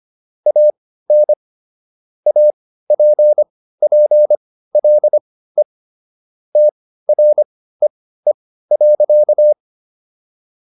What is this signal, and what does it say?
Morse code, heard as a keyed tone, about 25 wpm, spaced as for 15 wpm, 600 Hz. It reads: AN APPLE TREE.